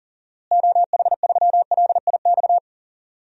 Morse code, heard as a keyed tone, about 40 wpm, 700 Hz. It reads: OH3LIX
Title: DXCC-100 Call Signs - 6 Characters - Repeated 2x - 40wpm